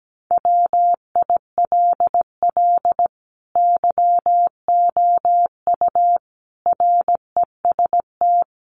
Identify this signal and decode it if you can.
Morse code, heard as a keyed tone, about 17 wpm, 700 Hz. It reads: WILL YOU REST
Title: Sentences from Top 500 Words 17wpm